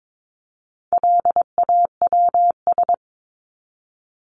Morse code, heard as a keyed tone, about 22 wpm, 700 Hz. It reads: LAWS
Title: ES:S Suffix - Top 100 Words with Base Words 22wpm